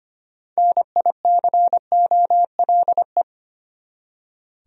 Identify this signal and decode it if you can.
Morse code, heard as a keyed tone, about 25 wpm, 700 Hz. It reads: NICOLE
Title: USA Common Names - 25wpm